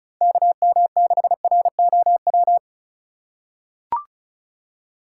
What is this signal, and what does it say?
Morse code, heard as a keyed tone, about 34 wpm, 700 Hz. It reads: KM6ROW
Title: USA Call Signs - 6 Characters - 8x Character Spacing - 35wpm